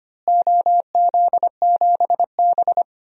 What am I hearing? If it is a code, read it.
Morse code, heard as a keyed tone, about 25 wpm, 700 Hz. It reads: OZ7B
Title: DXCC-100 Call Signs - 4 Characters - Repeated 3x - 25wpm